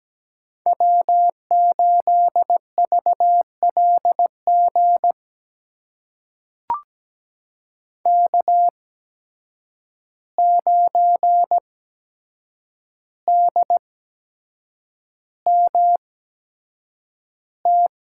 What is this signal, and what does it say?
Morse code, heard as a keyed tone, about 17 wpm, spaced as for 7 wpm, 700 Hz. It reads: W8VLG K9DMT
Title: USA Call Signs - 5 Characters - 8x Character Spacing - 17wpm